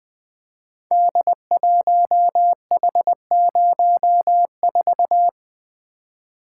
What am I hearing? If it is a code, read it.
Morse code, heard as a keyed tone, about 20 wpm, 700 Hz. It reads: D1H04
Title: Code Groups - 5 Letter:Numbers - 20wpm